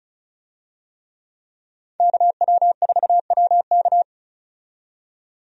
Morse code, heard as a keyed tone, about 36 wpm, 700 Hz. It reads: KW4WK
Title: USA Call Signs - 5 Characters - 35wpm